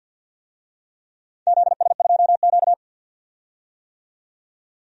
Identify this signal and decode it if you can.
Morse code, heard as a keyed tone, about 51 wpm, 700 Hz. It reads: ZS2Q
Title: DXCC-100 Call Signs - 4 Characters - 50wpm